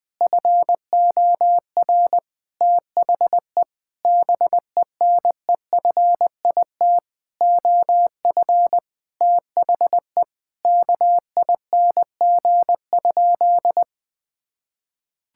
Morse code, heard as a keyed tone, about 20 wpm, 700 Hz. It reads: FOR THE BENEFIT OF THE KING?
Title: Sentences from Top 1000 Words - Review Part 6 - 20wpm